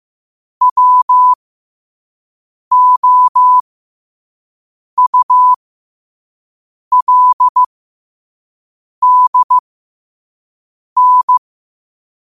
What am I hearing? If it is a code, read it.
Morse code, heard as a keyed tone, about 15 wpm, spaced as for 5 wpm, 1000 Hz. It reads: WOULDN
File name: qotdfiles / 5wpm